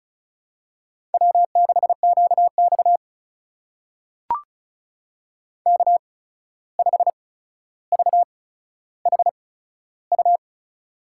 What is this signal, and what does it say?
Morse code, heard as a keyed tone, about 36 wpm, spaced as for 14 wpm, 700 Hz. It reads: W6QX K5VHU
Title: USA Call Signs - 8x Character Spacing - 35wpm